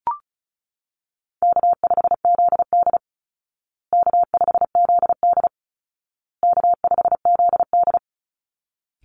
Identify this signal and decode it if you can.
Morse code, heard as a keyed tone, about 34 wpm, 700 Hz. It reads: K5ZD K5ZD K5ZD
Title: Easy Call Signs Repeated 3x - 35wpm